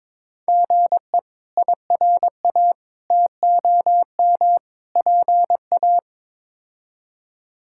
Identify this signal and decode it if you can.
Morse code, heard as a keyed tone, about 22 wpm, 700 Hz. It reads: GE IRA TOM PA
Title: K1USN SST QSO Practice - 22wpm